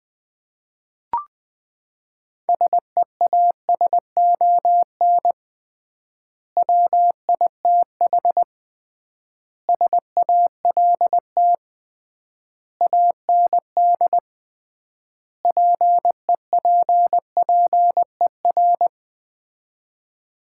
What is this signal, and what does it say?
Morse code, heard as a keyed tone, about 21 wpm, 700 Hz. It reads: SEASON WITH SALT AND PEPPER
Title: 3x Word-Spacing - Sets of 5 Words - 20wpm